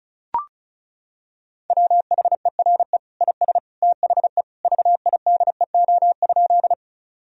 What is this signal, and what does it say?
Morse code, heard as a keyed tone, about 36 wpm, 700 Hz. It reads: WHERE IS THE VIDEO?
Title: Sentences from Top 1000 Words - Review Part 4 - 35wpm